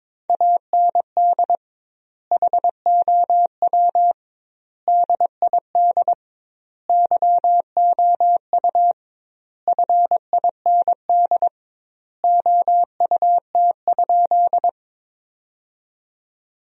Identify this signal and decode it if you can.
Morse code, heard as a keyed tone, about 22 wpm, 700 Hz. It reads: AND HOW DID YOU FIND OUT?
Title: 2x Word-Spacing - Sentences from Top 200 Words - 22wpm